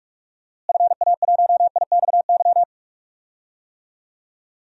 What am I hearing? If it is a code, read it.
Morse code, heard as a keyed tone, about 43 wpm, 700 Hz. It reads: FA1IXY
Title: DXCC-100 Call Signs - 6 Characters - Repeated 2x - 45wpm